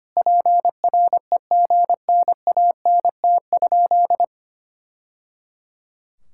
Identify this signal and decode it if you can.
Morse code, heard as a keyed tone, about 25 wpm, 700 Hz. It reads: PREGNANT?
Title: Sentences from Top 1000 Words - Review Part 6 - 25wpm